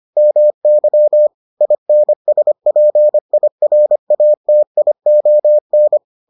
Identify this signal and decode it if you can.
Morse code, heard as a keyed tone, about 25 wpm, 600 Hz. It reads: MY INSPIRATION